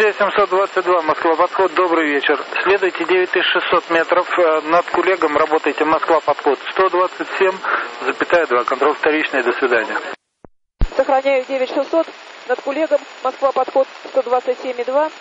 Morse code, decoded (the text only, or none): none